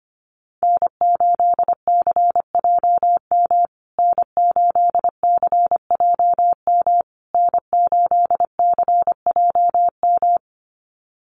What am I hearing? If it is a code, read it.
Morse code, heard as a keyed tone, about 25 wpm, 700 Hz. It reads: N8CJM N8CJM N8CJM
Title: USA Call Signs Repeated 3x 25wpm